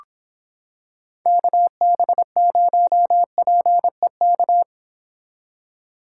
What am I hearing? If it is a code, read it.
Morse code, heard as a keyed tone, about 26 wpm, 700 Hz. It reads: KB0PEK